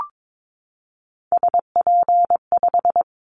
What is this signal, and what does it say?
Morse code, heard as a keyed tone, about 22 wpm, 700 Hz. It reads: SP5
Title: Callsign Prefixes for Top 25 DXCC - 22wpm